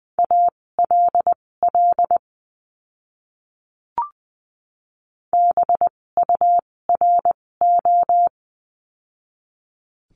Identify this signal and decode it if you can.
Morse code, heard as a keyed tone, about 20 wpm, spaced as for 16 wpm, 700 Hz. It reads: ALL BURO
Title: 256 Lesson 42a - Words 12-20wpm